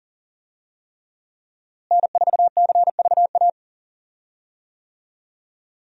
Morse code, heard as a keyed tone, about 40 wpm, 700 Hz. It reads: N4CVA
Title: USA Call Signs - 5 Characters - Repeated 3x - 40wpm